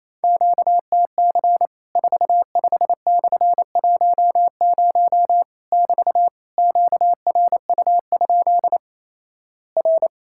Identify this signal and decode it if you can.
Morse code, heard as a keyed tone, about 28 wpm, 700 Hz. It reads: QTC 45/10 <BT> QRU? R
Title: Worked All Europe DX Contest - QTC Exchange - 28wpm